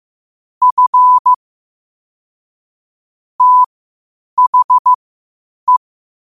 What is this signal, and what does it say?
Morse code, heard as a keyed tone, about 15 wpm, spaced as for 8 wpm, 1000 Hz. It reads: F THE
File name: qotdfiles / 8wpm